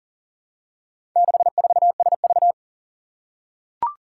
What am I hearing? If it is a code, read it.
Morse code, heard as a keyed tone, about 43 wpm, 700 Hz. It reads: 64SV
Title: Four Challenging Letter-Number Combinations 40wpm